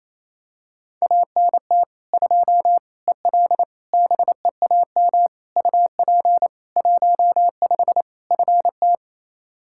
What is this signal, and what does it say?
Morse code, heard as a keyed tone, about 28 wpm, 700 Hz. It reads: ANT 2 EL BEAM UP 15 FT
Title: Practice QSO - Exchange 2 - Info - 28wpm